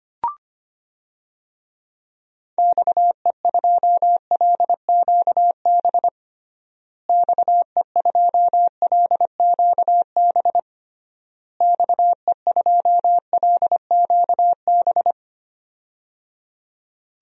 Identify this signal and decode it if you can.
Morse code, heard as a keyed tone, about 25 wpm, 700 Hz. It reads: XE2LQB XE2LQB XE2LQB
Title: DXCC-100 Call Signs - 6 Characters - Repeated 3x - 25wpm